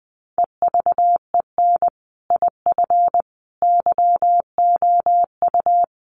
Morse code, heard as a keyed tone, about 20 wpm, 700 Hz. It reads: EVEN IF YOU
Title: Sets of 5 Words Encore 20wpm